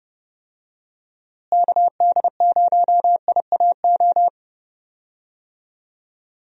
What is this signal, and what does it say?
Morse code, heard as a keyed tone, about 30 wpm, 700 Hz. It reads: KD0IAO
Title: USA Call Signs - 6 Characters - Repeated 3x - 30wpm